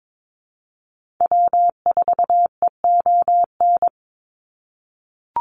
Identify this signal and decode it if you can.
Morse code, heard as a keyed tone, about 22 wpm, 700 Hz. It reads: W4EON
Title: NA SOTA Activator Callsigns 22wpm